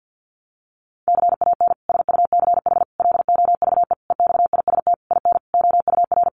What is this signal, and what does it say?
Morse code, heard as a keyed tone, about 50 wpm, 700 Hz. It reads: CAN SUCH LOVE EXIST IN OUR
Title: Sentences from Top 1100 Words 50wpm